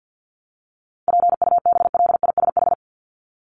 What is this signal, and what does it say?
Morse code, heard as a keyed tone, about 50 wpm, 700 Hz. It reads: PUBLISH
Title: ED Suffix - Top 100 Words with Base Words 50wpm